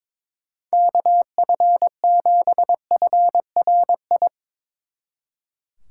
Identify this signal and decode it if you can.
Morse code, heard as a keyed tone, about 24 wpm, 700 Hz. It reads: KF7FRI
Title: Speed Racing - USA 2x3 Call Signs 22wpm